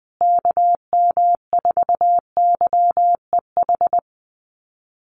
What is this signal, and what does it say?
Morse code, heard as a keyed tone, about 20 wpm, 700 Hz. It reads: KM4YEH
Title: USA Call Signs Repeated 2x 20wpm